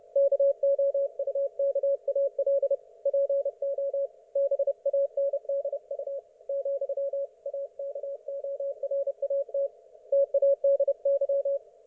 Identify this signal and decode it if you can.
Morse code, heard as a keyed tone, about 31 wpm, 560 Hz. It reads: KOUKAL PO BANDU ,AKORAT TADY
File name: Archiv / CUC179 / ok1fhi cul179